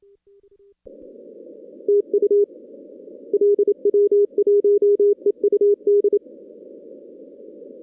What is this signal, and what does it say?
Morse code, heard as a keyed tone, about 27 wpm, 410 Hz. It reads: TU LW1EUD
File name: 2016 arrl dx cw / 2016-ARRL-DX-CW-K5TR- - 0-46